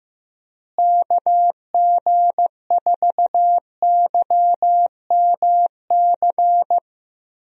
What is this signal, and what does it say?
Morse code, heard as a keyed tone, about 15 wpm, 700 Hz. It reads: KG4YMC
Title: USA Call Signs - 6 Characters - Repeated 2x - 15wpm